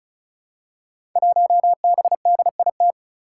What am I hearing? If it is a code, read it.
Morse code, heard as a keyed tone, about 36 wpm, 700 Hz. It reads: 1BDIT
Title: Five Challenging Letter-Number Combinations 35wpm